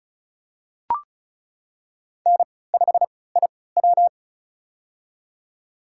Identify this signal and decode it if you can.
Morse code, heard as a keyed tone, about 36 wpm, spaced as for 20 wpm, 700 Hz. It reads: N5IW
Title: USA Call Signs - 3x Character Spacing - 35wpm